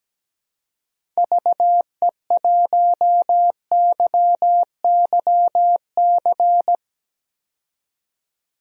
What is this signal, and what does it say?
Morse code, heard as a keyed tone, about 17 wpm, 700 Hz. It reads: VE1YYC
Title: DXCC-100 Call Signs - 6 Characters - 17wpm